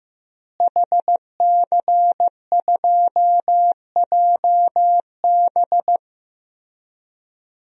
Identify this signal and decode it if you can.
Morse code, heard as a keyed tone, about 15 wpm, 700 Hz. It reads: HC2JB
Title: DXCC-100 Call Signs - 5 Characters - Repeated 2x - 15wpm